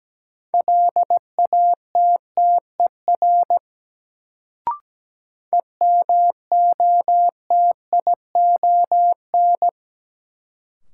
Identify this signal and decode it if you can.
Morse code, heard as a keyed tone, about 17 wpm, 700 Hz. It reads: LATTER EMOTION
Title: Sentences from Top 1600 Words - New Word Drill Alt - 17wpm